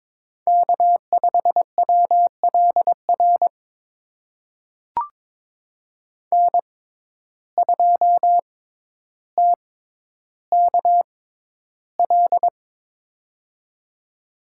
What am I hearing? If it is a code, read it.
Morse code, heard as a keyed tone, about 22 wpm, spaced as for 11 wpm, 700 Hz. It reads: K5WLR N2TKL
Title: USA Call Signs - 5 Characters - 6x Character Spacing - 22wpm